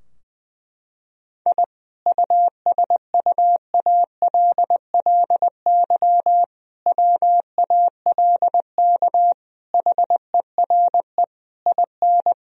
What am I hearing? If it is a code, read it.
Morse code, heard as a keyed tone, about 20 wpm, 700 Hz. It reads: I USUALLY WALK HERE IN
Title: Sentences from Top 1000 Words - Review Part 3 - 20wpm